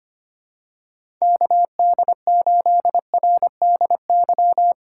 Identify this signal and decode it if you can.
Morse code, heard as a keyed tone, about 25 wpm, 700 Hz. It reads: KD8RDY
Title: USA Call Signs - 6 Characters - 2x Character Spacing - 25wpm